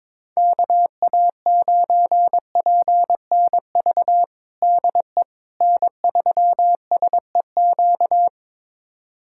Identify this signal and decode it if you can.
Morse code, heard as a keyed tone, about 22 wpm, 700 Hz. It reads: KA9PNV DE N3SEQ
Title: USA Callsign DE Callsign 22wpm